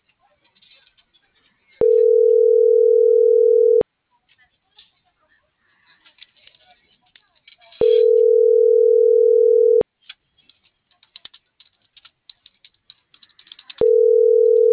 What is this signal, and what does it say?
Ringback tone (North America), 440 + 480 Hz.